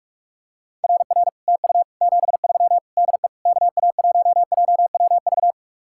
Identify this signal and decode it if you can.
Morse code, heard as a keyed tone, about 44 wpm, 700 Hz. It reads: RR TU 73 DE KA1JWV